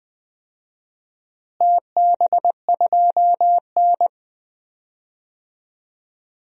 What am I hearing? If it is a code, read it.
Morse code, heard as a keyed tone, about 20 wpm, 700 Hz. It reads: TB2N
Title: DXCC-100 Call Signs - 4 Characters - Repeated 3x - 20wpm